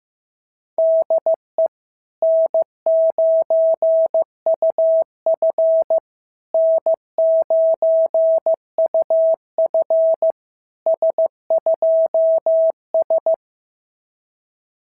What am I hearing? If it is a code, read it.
Morse code, heard as a keyed tone, about 14 wpm, 650 Hz. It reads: DE N9UF N9UF S2S
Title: SOTA QSO Practice 15wpm